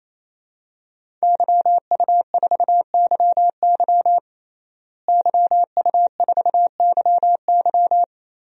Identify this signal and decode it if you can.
Morse code, heard as a keyed tone, about 28 wpm, 700 Hz. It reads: YU4YY YU4YY
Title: DXCC-100 Call Signs - 5 Characters - Repeated 3x - 28wpm